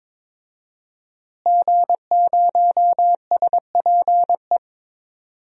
Morse code, heard as a keyed tone, about 22 wpm, 700 Hz. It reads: G0SPE
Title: DXCC-100 Call Signs - 5 Characters - Repeated 3x - 22wpm